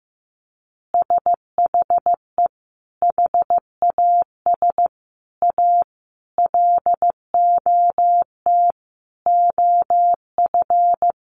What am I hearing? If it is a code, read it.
Morse code, heard as a keyed tone, about 15 wpm, 700 Hz. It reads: SHE HAS A LOT OF
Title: Additional Easy Phrases 15wpm